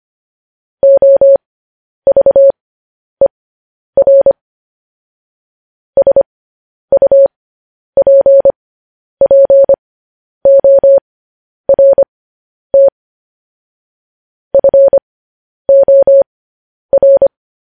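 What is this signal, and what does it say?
Morse code, heard as a keyed tone, about 25 wpm, spaced as for 10 wpm, 550 Hz. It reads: OVER SUPPORT FOR
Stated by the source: Morse code transmission of news headlines.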